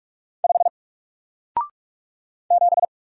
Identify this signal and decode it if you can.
Morse code, heard as a keyed tone, about 44 wpm, 700 Hz. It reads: 5 7
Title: Single Character (Letter:Number:Punctuation:Prosign) Alt 45wpm